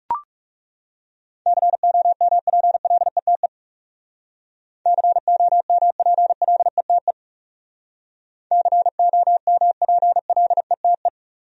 Morse code, heard as a keyed tone, about 39 wpm, 700 Hz. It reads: COMPLETE COMPLETE COMPLETE